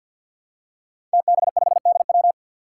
Morse code, heard as a keyed tone, about 52 wpm, 700 Hz. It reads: TB5DW
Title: Five Challenging Letter-Number Combinations 50wpm